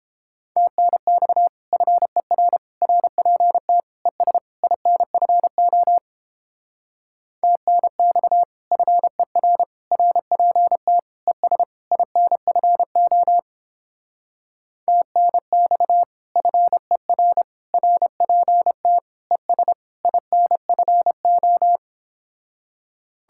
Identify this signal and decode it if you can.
Morse code, heard as a keyed tone, about 29 wpm, 700 Hz. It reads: TNX FER RPT ES INFO TNX FER RPT ES INFO TNX FER RPT ES INFO